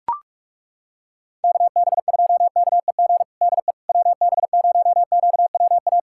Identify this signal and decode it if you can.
Morse code, heard as a keyed tone, about 42 wpm, 700 Hz. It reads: KB2KEG DE WB0QWA
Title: USA Callsign DE Callsign 45wpm